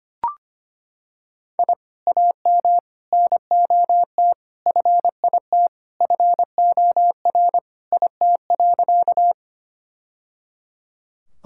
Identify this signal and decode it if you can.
Morse code, heard as a keyed tone, about 25 wpm, 700 Hz. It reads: I AM NOT FIT FOR IT.
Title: Sentences from Top 1000 Words - Review Part 5 - 25wpm